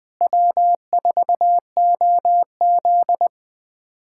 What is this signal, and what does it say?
Morse code, heard as a keyed tone, about 20 wpm, 700 Hz. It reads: W4OZ
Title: USA Callsign DE Callsign - 4 Characters - 20wpm